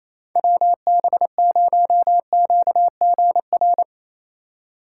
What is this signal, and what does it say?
Morse code, heard as a keyed tone, about 28 wpm, 700 Hz. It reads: WB0QGR
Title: USA Call Signs - 6 Characters - 4x Character Spacing - 28wpm